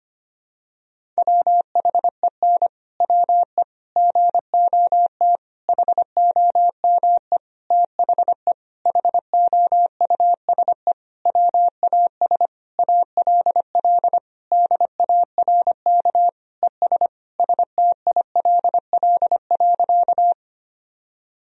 Morse code, heard as a keyed tone, about 25 wpm, 700 Hz. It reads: WHEN WE GOT HOME THE HOUSE WAS ALL DARK ES STILL.
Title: Sentences from Top 500 Words and Abbreviations 25wpm